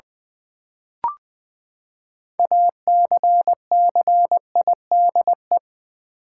Morse code, heard as a keyed tone, about 20 wpm, 700 Hz. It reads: ACCIDE
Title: Build A Word - 6 Letters At A Time - 12-Letter Words - 20wpm